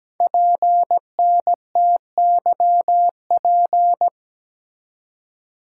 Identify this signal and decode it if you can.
Morse code, heard as a keyed tone, about 17 wpm, 700 Hz. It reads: PNTYP
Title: Code Groups - 5 Letter:Numbers - 17wpm